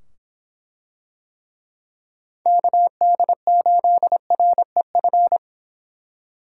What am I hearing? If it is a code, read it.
Morse code, heard as a keyed tone, about 24 wpm, 700 Hz. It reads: KD8REF